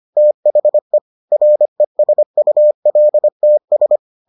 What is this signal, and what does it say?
Morse code, heard as a keyed tone, about 25 wpm, 600 Hz. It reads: THE RESULTS